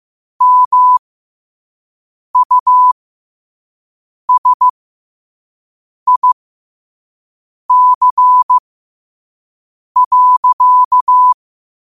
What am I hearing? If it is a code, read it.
Morse code, heard as a keyed tone, about 15 wpm, spaced as for 5 wpm, 1000 Hz. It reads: MUSIC.
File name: qotdfiles / 5wpm